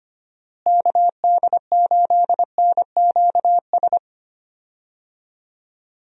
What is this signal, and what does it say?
Morse code, heard as a keyed tone, about 25 wpm, 700 Hz. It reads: KD8NQS
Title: USA Call Signs - 6 Characters - Repeated 3x - 25wpm